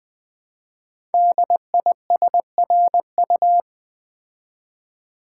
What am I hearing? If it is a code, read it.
Morse code, heard as a keyed tone, about 20 wpm, 700 Hz. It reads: DISRU
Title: Build A Word - 5 Letters At A Time - 10-Letter Words - 20wpm